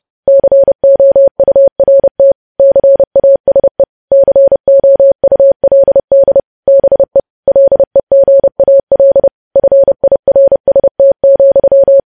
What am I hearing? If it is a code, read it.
Morse code, heard as a keyed tone, about 30 wpm, 550 Hz. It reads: COURT CASE COULD BE LEGAL FIRST,
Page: news headlines in morse code at 30 wpm